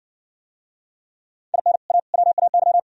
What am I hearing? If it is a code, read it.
Morse code, heard as a keyed tone, about 60 wpm, 700 Hz. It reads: IN A WAY